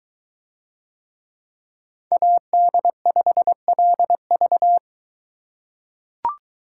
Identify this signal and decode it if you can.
Morse code, heard as a keyed tone, about 21 wpm, 700 Hz. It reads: AD5LV